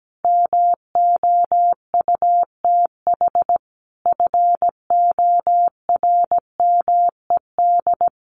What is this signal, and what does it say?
Morse code, heard as a keyed tone, about 17 wpm, 700 Hz. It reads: MOUTH FORMED
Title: A Sound Of Thunder 17wpm